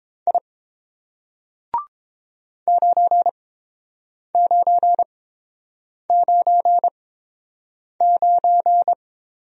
Morse code, heard as a keyed tone, about 30 wpm, spaced as for 7 wpm, 700 Hz. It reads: I 9999